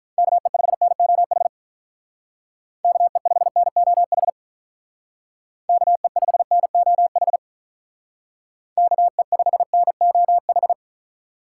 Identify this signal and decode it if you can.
Morse code, heard as a keyed tone, about 44 wpm, 700 Hz. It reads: KE5NOH KE5NOH KE5NOH KE5NOH